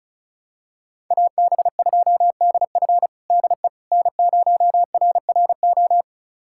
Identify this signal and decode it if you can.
Morse code, heard as a keyed tone, about 36 wpm, 700 Hz. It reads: AB2DF DE N0RRO
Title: USA Callsign DE Callsign 35wpm